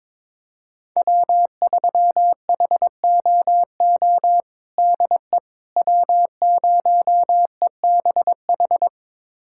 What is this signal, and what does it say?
Morse code, heard as a keyed tone, about 21 wpm, 700 Hz. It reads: W3HOO DE W0EBH
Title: USA Callsign DE Callsign - 5 Characters - 22wpm